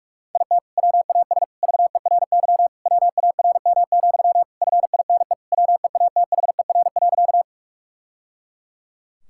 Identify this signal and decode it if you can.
Morse code, heard as a keyed tone, about 46 wpm, 700 Hz. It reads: IT WAS VERY WARM, FINE WEATHER.
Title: Sentences from Top 1000 Words - Review Part 5 - 45wpm